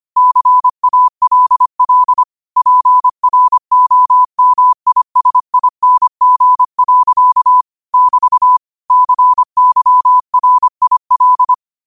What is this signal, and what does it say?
Morse code, heard as a keyed tone, about 25 wpm, 1000 Hz. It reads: CALL PROMISING. <BT> CYRIL